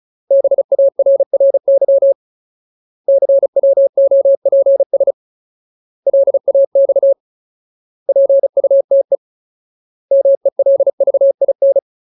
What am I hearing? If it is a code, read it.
Morse code, heard as a keyed tone, about 36 wpm, 550 Hz. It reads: BARRY CWOPS LAX PUTE MELVIN
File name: Ultimate Word List / 35WPM Ultimate